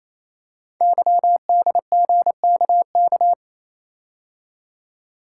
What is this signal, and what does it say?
Morse code, heard as a keyed tone, about 27 wpm, 700 Hz. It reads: YDGKK
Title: Code Groups - 5 Letters - 28wpm